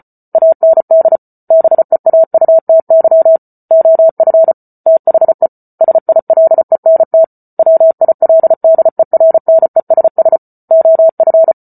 Morse code, heard as a keyed tone, about 34 wpm, 650 Hz. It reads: AND BEAUTY OF THE SILENT WILDERNESS OF